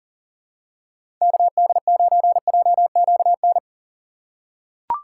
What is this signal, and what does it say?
Morse code, heard as a keyed tone, about 40 wpm, 700 Hz. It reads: KD9JQN
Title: USA Call Signs - 6 Characters - 2x Character Spacing - 40wpm